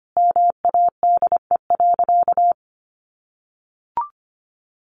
Morse code, heard as a keyed tone, about 25 wpm, 700 Hz. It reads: MADE.
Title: Sentences from Top 1200 Words 25wpm